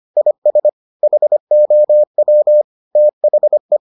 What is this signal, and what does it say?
Morse code, heard as a keyed tone, about 25 wpm, 600 Hz. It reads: IS HOW THE